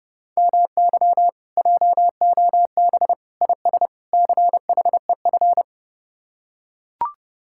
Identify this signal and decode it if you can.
Morse code, heard as a keyed tone, about 31 wpm, 700 Hz. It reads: MY JOB IS CHEF